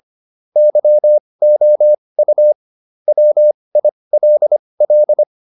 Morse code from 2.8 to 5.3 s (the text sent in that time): WILL